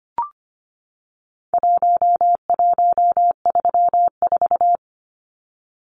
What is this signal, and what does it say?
Morse code, heard as a keyed tone, about 25 wpm, 700 Hz. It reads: 1134
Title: Numbers - Contest Style - 4-Digit 25wpm